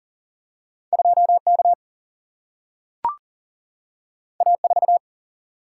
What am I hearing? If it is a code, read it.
Morse code, heard as a keyed tone, about 39 wpm, 700 Hz. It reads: 2K A4
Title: Single-Letter Single-Number 40wpm